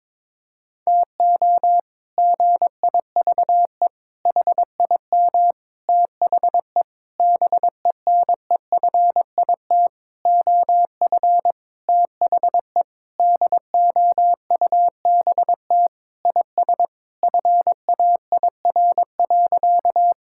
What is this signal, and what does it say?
Morse code, heard as a keyed tone, about 22 wpm, 700 Hz. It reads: TO GIVE HIM THE BENEFIT OF THE DOUBT IS FAIR.
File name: practice / 22wpm / Sentences with 8-Word Phrases - 22wpm